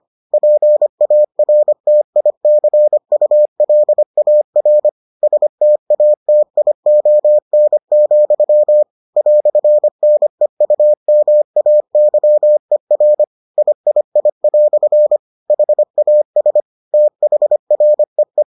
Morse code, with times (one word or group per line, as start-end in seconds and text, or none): 0.3-4.9 PARTICULAR
5.2-8.8 STATION,
9.2-13.2 "NEUMAYER
13.6-15.2 III"
15.5-16.6 HAS
16.9-18.4 THREE